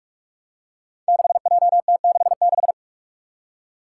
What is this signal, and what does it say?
Morse code, heard as a keyed tone, about 44 wpm, 700 Hz. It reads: 6JT66
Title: Five Challenging Letter-Number Combinations 45wpm